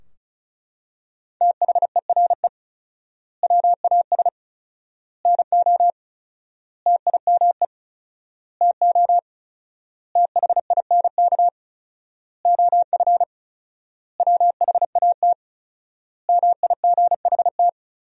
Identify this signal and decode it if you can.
Morse code, heard as a keyed tone, about 36 wpm, 700 Hz. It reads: THERE WAS NO TIME TO THINK OF WHAT MIGHT
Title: 4x Word-Spacing - Sentences from Top 400 Words - 35wpm